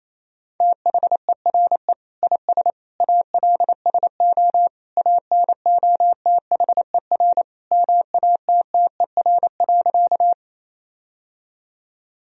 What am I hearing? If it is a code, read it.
Morse code, heard as a keyed tone, about 28 wpm, 700 Hz. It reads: THERE IS ALSO ANOTHER MATTER.
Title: Sentences from Top 1000 Words - Review Part 1 - 28wpm